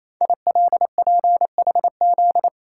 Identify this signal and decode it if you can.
Morse code, heard as a keyed tone, about 27 wpm, 700 Hz. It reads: ILPHZ
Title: Code Groups - 5 Letter:Numbers - 28wpm